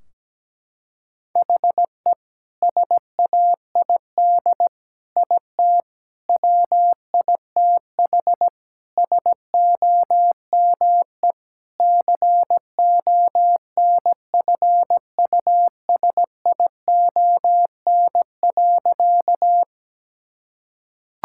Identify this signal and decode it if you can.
Morse code, heard as a keyed tone, about 17 wpm, 700 Hz. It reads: HE SAID IT WITH SOME CONFUSION.
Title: Sentences from Top 1600 Words 17wpm